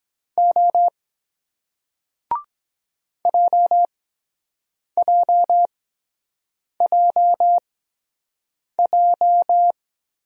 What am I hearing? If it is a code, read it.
Morse code, heard as a keyed tone, about 23 wpm, spaced as for 7 wpm, 700 Hz. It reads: O JJJJ